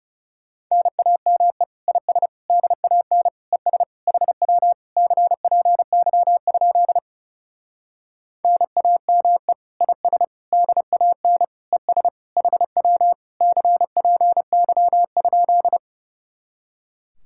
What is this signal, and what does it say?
Morse code, heard as a keyed tone, about 35 wpm, 700 Hz. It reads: NAME IS DAN ES HW CPY? NAME IS DAN ES HW CPY?
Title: Speed-Racing Common QSO Phrases Encore - 30wpm